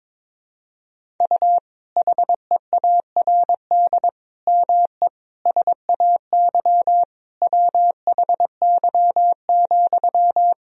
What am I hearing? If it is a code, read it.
Morse code, heard as a keyed tone, about 22 wpm, 700 Hz. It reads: U HEARD ME SAY WHY,
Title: Sentences from Top 500 Words and Abbreviations 22wpm